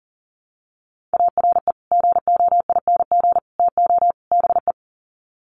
Morse code, heard as a keyed tone, about 40 wpm, 700 Hz. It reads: ARE GOING TO BE
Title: Sets of 4 Words 40wpm